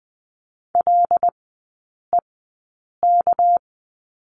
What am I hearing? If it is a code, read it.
Morse code, heard as a keyed tone, about 20 wpm, spaced as for 8 wpm, 700 Hz. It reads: LEK
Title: 198 Lesson 32e - Letter k - Three Characters 20wpm